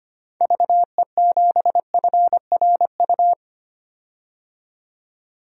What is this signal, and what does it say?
Morse code, heard as a keyed tone, about 25 wpm, 700 Hz. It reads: VE7FRU
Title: DXCC-100 Call Signs - 6 Characters - 25wpm